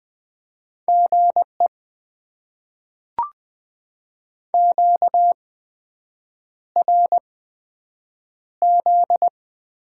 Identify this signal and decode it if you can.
Morse code, heard as a keyed tone, about 20 wpm, spaced as for 7 wpm, 700 Hz. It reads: GE QRZ
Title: Top 90 QSO Elements - 8x Character Spacing - 20wpm